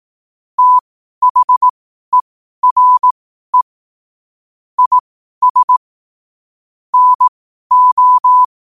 Morse code, heard as a keyed tone, about 18 wpm, spaced as for 12 wpm, 1000 Hz. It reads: THERE IS NO
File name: 12wpm